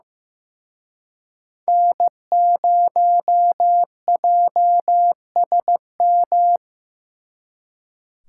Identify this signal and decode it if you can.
Morse code, heard as a keyed tone, about 18 wpm, 700 Hz. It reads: N0JSM
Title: Speed Racing - USA Call Signs 15wpm